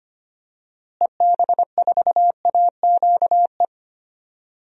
Morse code, heard as a keyed tone, about 25 wpm, 700 Hz. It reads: EB4AQE
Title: DXCC-100 Call Signs - 6 Characters - Repeated 2x - 25wpm